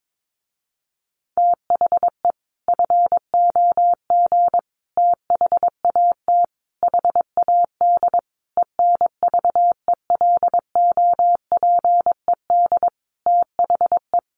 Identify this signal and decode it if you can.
Morse code, heard as a keyed tone, about 22 wpm, 700 Hz. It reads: THE FOG THAT HAD ENVELOPED THE
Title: A Sound Of Thunder 22wpm